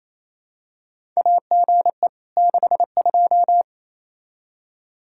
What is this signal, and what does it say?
Morse code, heard as a keyed tone, about 28 wpm, 700 Hz. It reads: AGE 62
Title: Practice QSO - Exchange 3 - More Info - 28wpm